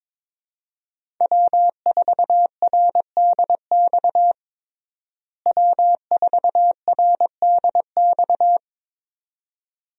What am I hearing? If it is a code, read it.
Morse code, heard as a keyed tone, about 22 wpm, 700 Hz. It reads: W4RDX W4RDX
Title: USA Call Signs - 5 Characters - Repeated 2x - 22wpm